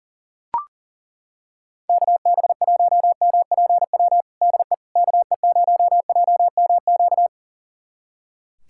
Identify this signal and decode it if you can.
Morse code, heard as a keyed tone, about 40 wpm, 700 Hz. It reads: KB1MPW DE KE0JMQ
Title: USA Callsign DE Callsign - 6 Characters - 40wpm